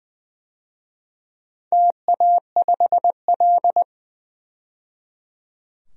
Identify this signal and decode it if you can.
Morse code, heard as a keyed tone, about 20 wpm, 700 Hz. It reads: TA5L
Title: DXCC-100 Call Signs - 4 Characters - 20wpm